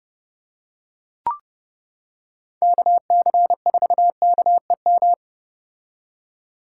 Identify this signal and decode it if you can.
Morse code, heard as a keyed tone, about 29 wpm, 700 Hz. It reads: KC4KEM